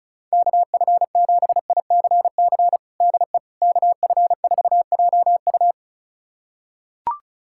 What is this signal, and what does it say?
Morse code, heard as a keyed tone, about 34 wpm, 700 Hz. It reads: KF7ICC DE KF4JU
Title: USA Callsign DE Callsign 35wpm